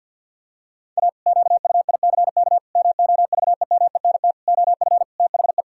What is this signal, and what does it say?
Morse code, heard as a keyed tone, about 47 wpm, 700 Hz. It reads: A QUICK MOVEMENT OF THE
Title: Pangram Challenge v2 - 50wpm